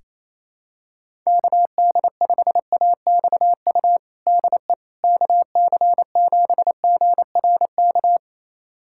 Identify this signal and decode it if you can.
Morse code, heard as a keyed tone, about 28 wpm, 700 Hz. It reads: KD5AXU DE KC7GRK
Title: USA Callsign DE Callsign - 6 Characters - 28wpm